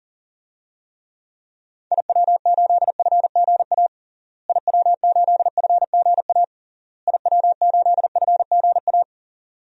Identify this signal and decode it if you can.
Morse code, heard as a keyed tone, about 37 wpm, 700 Hz. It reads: IW8FGA IW8FGA IW8FGA
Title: DXCC-100 Call Signs - 6 Characters - Repeated 3x - 40wpm